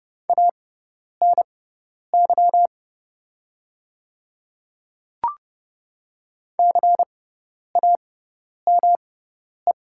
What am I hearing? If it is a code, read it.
Morse code, heard as a keyed tone, about 30 wpm, spaced as for 10 wpm, 690 Hz. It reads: ANY CAME
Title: 6x Character Spacing - Top 200 Words - 30wpm